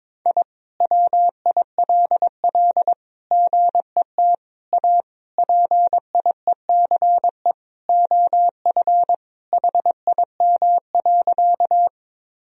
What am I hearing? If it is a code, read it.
Morse code, heard as a keyed tone, about 22 wpm, 700 Hz. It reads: I WILL GET A PIECE OF HIM.
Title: Sentences from Top 1000 Words - Review Part 2 - 22wpm